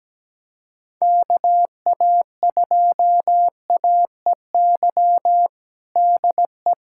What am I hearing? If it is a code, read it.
Morse code, heard as a keyed tone, about 17 wpm, 700 Hz. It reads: KA2AEY DE
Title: USA Callsign DE Callsign - 6 Characters - 17wpm